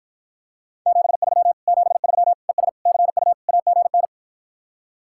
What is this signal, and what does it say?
Morse code, heard as a keyed tone, about 54 wpm, 700 Hz. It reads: 73 73 ES CU AGN